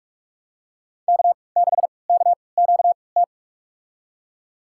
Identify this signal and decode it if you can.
Morse code, heard as a keyed tone, about 46 wpm, spaced as for 25 wpm, 700 Hz. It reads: K6KQT
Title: USA Call Signs - 5 Characters - 3x Character Spacing - 45wpm